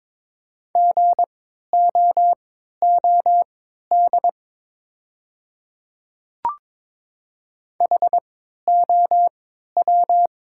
Morse code, heard as a keyed tone, about 22 wpm, spaced as for 12 wpm, 690 Hz. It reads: GOOD HOW
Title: 3x Character Spacing - Top 500 Words - 22wpm